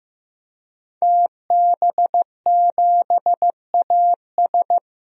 Morse code, heard as a keyed tone, about 15 wpm, 700 Hz. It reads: TB7AS
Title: DXCC-100 Call Signs - 5 Characters - 15wpm